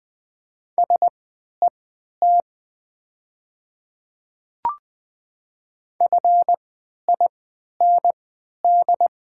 Morse code, heard as a keyed tone, about 20 wpm, spaced as for 11 wpm, 700 Hz. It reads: SET FIND
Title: 3x Character Spacing - Top 200 Words - 20wpm